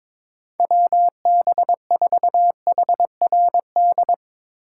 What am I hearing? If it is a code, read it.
Morse code, heard as a keyed tone, about 22 wpm, 700 Hz. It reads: WB4HRD
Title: USA Call Signs - 6 Characters - 2x Character Spacing - 22wpm